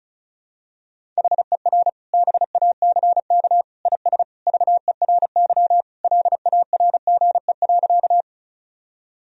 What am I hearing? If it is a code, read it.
Morse code, heard as a keyed tone, about 36 wpm, 700 Hz. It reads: HER BACK IS VERY LARGE.
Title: Sentences from Top 1000 Words - Review Part 1 - 35wpm